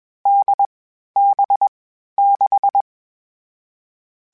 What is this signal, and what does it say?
Morse code, heard as a keyed tone, about 21 wpm, 800 Hz. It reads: D B 6
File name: Difficult Combos / Difficult Combos-mp3s / NDB NDB6 @20